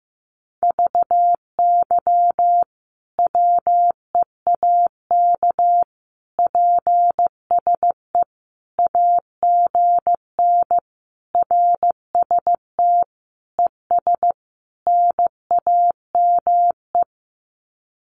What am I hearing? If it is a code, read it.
Morse code, heard as a keyed tone, about 15 wpm, 700 Hz. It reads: VY WEAK PSE AGN RST ES NAME
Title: Common QSO Phrases 15wpm